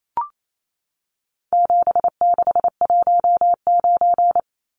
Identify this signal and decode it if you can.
Morse code, heard as a keyed tone, about 28 wpm, 700 Hz. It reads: 7619
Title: Numbers - 4-Digit 28wpm